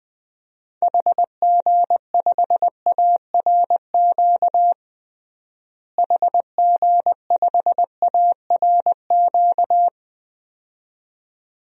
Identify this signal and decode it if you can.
Morse code, heard as a keyed tone, about 20 wpm, 700 Hz. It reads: HG5ARQ HG5ARQ
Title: DXCC-100 Call Signs - 6 Characters - Repeated 2x - 20wpm